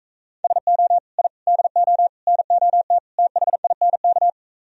Morse code, heard as a keyed tone, about 44 wpm, 700 Hz. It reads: SO I DO NOT THINK